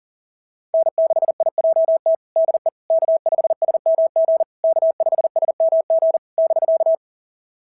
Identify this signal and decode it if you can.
Morse code, heard as a keyed tone, about 39 wpm, 660 Hz. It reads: N6IJT DE K5SMG K5SMG <BK>
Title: SOTA QSO Practice 40wpm